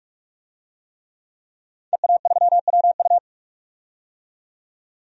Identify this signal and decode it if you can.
Morse code, heard as a keyed tone, about 44 wpm, 700 Hz. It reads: EA3WU
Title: DXCC-100 Call Signs - 5 Characters - 45wpm